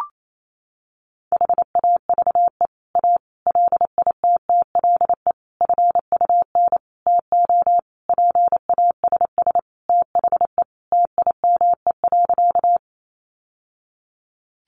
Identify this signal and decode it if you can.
Morse code, heard as a keyed tone, about 28 wpm, 700 Hz. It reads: HAVE A LITTLE FUN TO PASS THE TIME.
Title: Sentences from Top 700 Words 28wpm